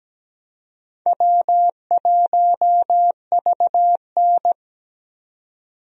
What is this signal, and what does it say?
Morse code, heard as a keyed tone, about 17 wpm, 700 Hz. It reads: W1VN
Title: USA Call Signs - 4 Characters - 17wpm